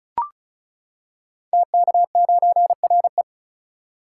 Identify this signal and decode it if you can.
Morse code, heard as a keyed tone, about 36 wpm, 690 Hz. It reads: TK9RE
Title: Code Groups - 5 Letter:Numbers - 35wpm